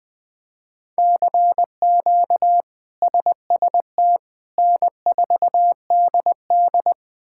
Morse code, heard as a keyed tone, about 21 wpm, 700 Hz. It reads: CQ SST N4DD
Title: K1USN SST QSO Practice - 20wpm